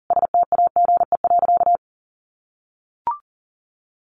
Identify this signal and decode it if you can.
Morse code, heard as a keyed tone, about 40 wpm, 700 Hz. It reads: STAGE.
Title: Sentences from Top 1200 Words 40wpm